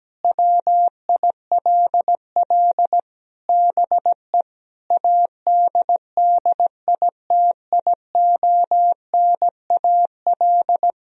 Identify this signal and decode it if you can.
Morse code, heard as a keyed tone, about 17 wpm, 700 Hz. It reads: WILL BE ADDITIONAL
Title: Sentences from Top 1000 Words - Review Part 4 - 17wpm